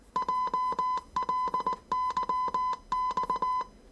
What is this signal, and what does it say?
Morse code, heard as a keyed tone, about 19 wpm, 1010 Hz. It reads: JLYX
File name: 业余无线电 / ●未分类 / 字码80